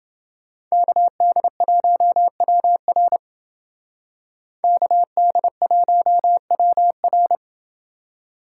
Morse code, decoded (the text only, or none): KD1WR KD1WR